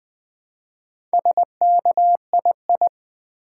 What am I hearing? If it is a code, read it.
Morse code, heard as a keyed tone, about 20 wpm, 700 Hz. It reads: SKII
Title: Build A Word - 2 Letters At A Time - 6-Letter Words - 20wpm